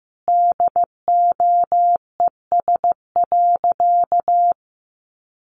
Morse code, heard as a keyed tone, about 15 wpm, 700 Hz. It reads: DOES.
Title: Sentences from Top 900 Words 15wpm